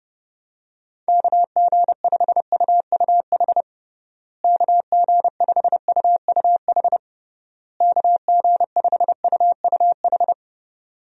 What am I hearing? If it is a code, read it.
Morse code, heard as a keyed tone, about 30 wpm, 700 Hz. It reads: KG5UUH KG5UUH KG5UUH
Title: USA Call Signs - 6 Characters - Repeated 3x - 30wpm